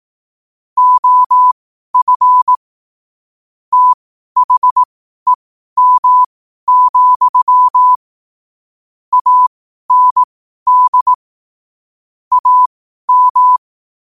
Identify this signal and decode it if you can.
Morse code, heard as a keyed tone, about 18 wpm, spaced as for 12 wpm, 1000 Hz. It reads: OF THEM, AND AM
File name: qotdfiles / 12wpm